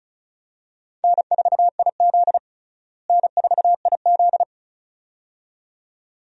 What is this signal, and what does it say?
Morse code, heard as a keyed tone, about 36 wpm, 700 Hz. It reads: N4IZ N4IZ
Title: USA Call Signs - 4 Characters - Repeated 2x - 35wpm